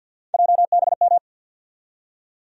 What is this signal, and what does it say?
Morse code, heard as a keyed tone, about 47 wpm, 700 Hz. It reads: JBM
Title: USA Call Sign Postfixes - 3 Letters 50wpm